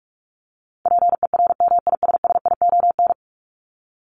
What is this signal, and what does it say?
Morse code, heard as a keyed tone, about 44 wpm, 700 Hz. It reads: PERMISSION
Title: Top 1501-1750 Words 45wpm